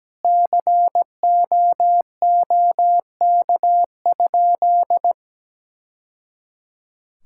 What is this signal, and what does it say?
Morse code, heard as a keyed tone, about 17 wpm, 700 Hz. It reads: COOK?
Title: Sentences from Top 1000 Words - Review Part 6 - 17wpm